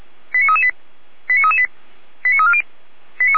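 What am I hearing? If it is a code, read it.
Five-tone selective call (ZVEI), heard as quick runs of five tones, tones of about 70 ms. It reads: 89338 89338 89340
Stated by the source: ZVEI five-tone selective call received by SDR.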